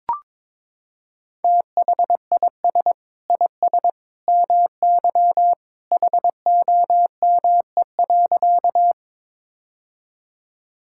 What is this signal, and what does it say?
Morse code, heard as a keyed tone, about 22 wpm, 700 Hz. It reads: THIS IS MY HOME.
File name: practice / 22wpm / Sentences with 2-Word Phrases - 22wpm